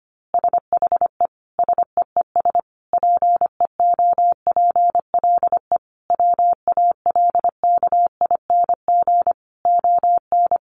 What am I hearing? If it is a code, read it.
Morse code, heard as a keyed tone, about 25 wpm, 700 Hz. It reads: SHE SEES PEOPLE WALKING ON
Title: Easy Phrases 25wpm